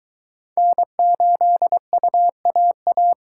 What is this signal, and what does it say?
Morse code, heard as a keyed tone, about 22 wpm, 700 Hz. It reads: N8UAA